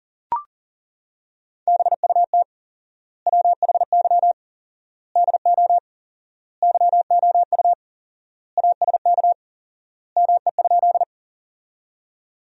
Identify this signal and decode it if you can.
Morse code, heard as a keyed tone, about 40 wpm, 700 Hz. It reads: BUT WHY DO YOU ASK ME?
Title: 4x Word-Spacing - Sentences from Top 200 Words - 40wpm